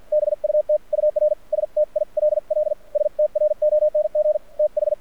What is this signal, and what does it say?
Morse code, heard as a keyed tone, about 48 wpm, spaced as for 42 wpm, 600 Hz. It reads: BUT UR STILL STRONG TH
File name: archived / CRSnet